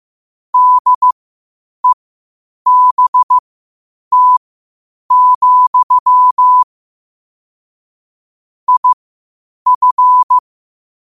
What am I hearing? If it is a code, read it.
Morse code, heard as a keyed tone, about 15 wpm, spaced as for 8 wpm, 1000 Hz. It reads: DEBT, IF
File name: 8wpm